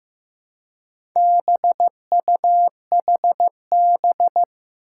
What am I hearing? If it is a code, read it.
Morse code, heard as a keyed tone, about 15 wpm, 700 Hz. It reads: BUHB
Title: Four Challenging Letter-Number Combinations 15wpm